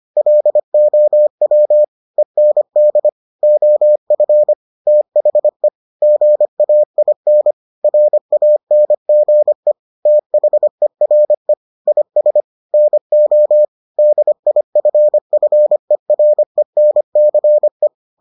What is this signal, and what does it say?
Morse code, heard as a keyed tone, about 25 wpm, 600 Hz. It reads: LOW END OF THE GAIN RANGE THERE IS NO DIFFERENCE